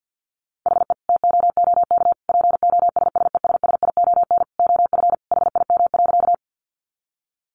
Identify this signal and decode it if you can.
Morse code, heard as a keyed tone, about 50 wpm, 700 Hz. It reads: HE TOOK POSSESSION OF HIM.
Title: Sentences from Top 1400 Words 50wpm